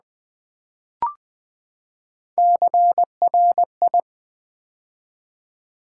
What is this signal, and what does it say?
Morse code, heard as a keyed tone, about 20 wpm, 700 Hz. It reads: CRI
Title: Build A Word - 3 Letters At A Time - 9-Letter Words - 20wpm